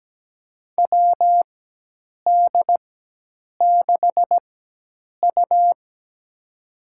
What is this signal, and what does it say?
Morse code, heard as a keyed tone, about 17 wpm, spaced as for 8 wpm, 700 Hz. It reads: WD6U
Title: USA Call Signs - 4 Characters - 4x Character Spacing - 17wpm